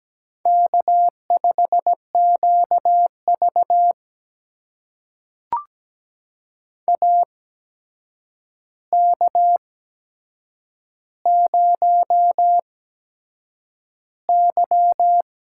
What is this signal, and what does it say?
Morse code, heard as a keyed tone, about 17 wpm, spaced as for 7 wpm, 700 Hz. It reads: K5QV AK0Y
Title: USA Call Signs - 4 Characters - 8x Character Spacing - 17wpm